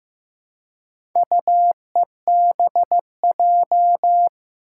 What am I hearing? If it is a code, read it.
Morse code, heard as a keyed tone, about 15 wpm, 700 Hz. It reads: UEBJ
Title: Four Challenging Letter-Number Combinations 15wpm